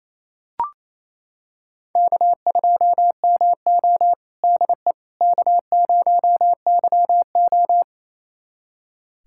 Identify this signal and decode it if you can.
Morse code, heard as a keyed tone, about 28 wpm, 700 Hz. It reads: K2MO DE K0YO
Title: USA Callsign DE Callsign - 4 Characters - 28wpm